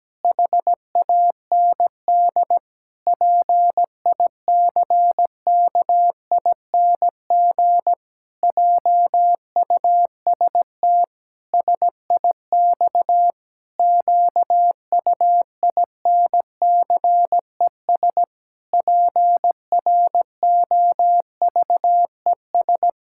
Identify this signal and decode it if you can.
Morse code, heard as a keyed tone, about 17 wpm, 700 Hz. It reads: HAND PICKING JUST SIX QUINCES PROVES
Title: Pangram Challenge v2 - 17wpm